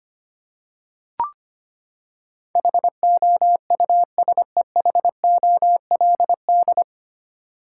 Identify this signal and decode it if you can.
Morse code, heard as a keyed tone, about 25 wpm, 700 Hz. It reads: HOUSEHOLD
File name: Nine Letter Words - 25wpm